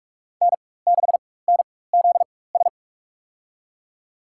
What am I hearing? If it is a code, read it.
Morse code, heard as a keyed tone, about 44 wpm, spaced as for 21 wpm, 700 Hz. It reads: N6NZS
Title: USA Call Signs - 4x Character Spacing - 45wpm